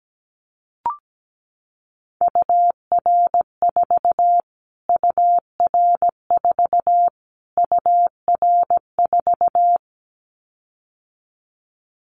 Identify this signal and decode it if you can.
Morse code, heard as a keyed tone, about 17 wpm, 700 Hz. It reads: UR4 UR4 UR4
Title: Callsign Prefixes for Top 25 DXCC - Repeated 3x - 17wpm